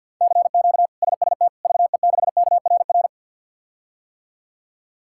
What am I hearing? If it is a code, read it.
Morse code, heard as a keyed tone, about 53 wpm, 700 Hz. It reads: CQ SST VE6KRR
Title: K1USN SST QSO Practice - 50wpm